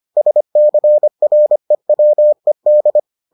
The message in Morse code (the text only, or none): SCREWED